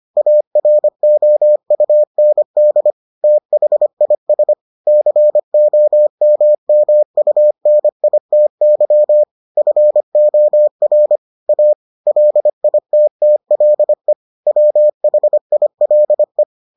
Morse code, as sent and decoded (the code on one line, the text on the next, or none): .- .-. --- ..- -. -.. / - .... .. ... / -.-. --- -- -- ..- -. .. - -.-- / ..-. --- .-. / .- / .-.. .. - - .-.. / .-- .... .. .-.. .
AROUND THIS COMMUNITY FOR A LITTLE WHILE